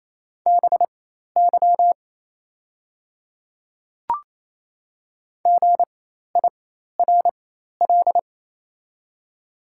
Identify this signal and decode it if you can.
Morse code, heard as a keyed tone, about 27 wpm, spaced as for 13 wpm, 690 Hz. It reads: BY GIRL
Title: 4x Character Spacing - Top 500 Words - 28wpm